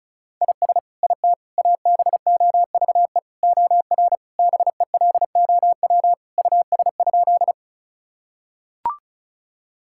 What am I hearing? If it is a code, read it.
Morse code, heard as a keyed tone, about 34 wpm, 700 Hz. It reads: IS IT ABOVE OR BELOW US?
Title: Sentences from Top 1000 Words - Review Part 5 - 35wpm